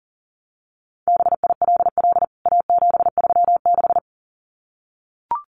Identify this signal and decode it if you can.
Morse code, heard as a keyed tone, about 40 wpm, 700 Hz. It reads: BILL A736
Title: CWT Exchange 40wpm